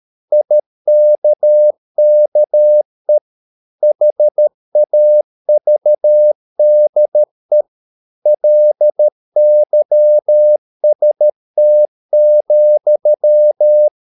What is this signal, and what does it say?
Morse code, heard as a keyed tone, about 13 wpm, 600 Hz. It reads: IKKE HAVDE LYST,
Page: Ferie 13 wpm